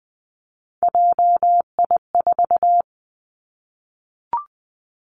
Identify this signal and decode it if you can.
Morse code, heard as a keyed tone, about 20 wpm, 700 Hz. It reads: JI4
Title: Three Challenging Letter-Number Combinations 20wpm